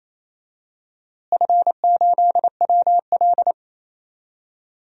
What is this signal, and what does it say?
Morse code, heard as a keyed tone, about 28 wpm, 700 Hz. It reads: F8WL
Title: DXCC-100 Call Signs - 4 Characters - Repeated 2x - 28wpm